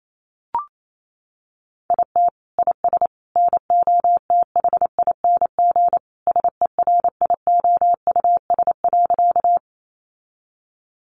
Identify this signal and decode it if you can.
Morse code, heard as a keyed tone, about 28 wpm, 700 Hz. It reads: IT IS NOTHING SERIOUS.
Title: Sentences from Top 1100 Words 28wpm